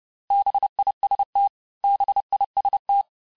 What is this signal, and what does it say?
Morse code, heard as a keyed tone, about 30 wpm, 800 Hz. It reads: BIST BIST
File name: cours-du-mercredi / audios / cours-854-mots-vitesse-28-mpm